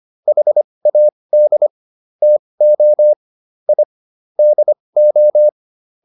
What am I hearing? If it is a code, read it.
Morse code, heard as a keyed tone, about 25 wpm, spaced as for 20 wpm, 600 Hz. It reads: HAD TO I DO